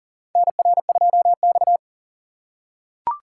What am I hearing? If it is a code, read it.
Morse code, heard as a keyed tone, about 40 wpm, 700 Hz. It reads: NR2X
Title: USA Call Signs - 4 Characters - 6x Character Spacing - 40wpm